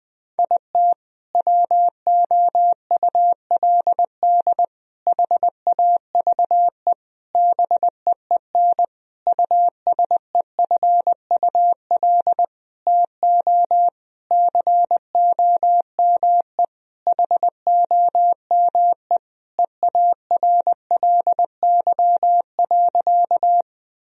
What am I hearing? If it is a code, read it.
Morse code, heard as a keyed tone, about 20 wpm, 700 Hz. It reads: IT WOULD HAVE BEEN USEFUL TO COME HOME EARLY.
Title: Sentences from Top 1000 Words - Review Part 4 - 20wpm